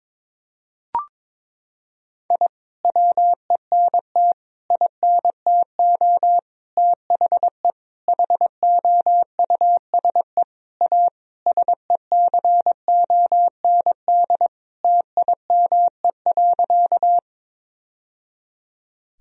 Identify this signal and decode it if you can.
Morse code, heard as a keyed tone, about 22 wpm, 700 Hz. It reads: I WENT INTO THE HOUSE A SECOND TIME.
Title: Sentences from Top 1000 Words - Review Part 1 - 22wpm